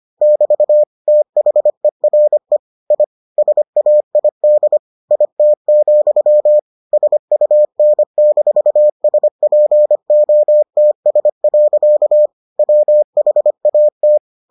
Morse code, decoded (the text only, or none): <BT> THERE I SAID IT, SUN-SPOTS. WHAT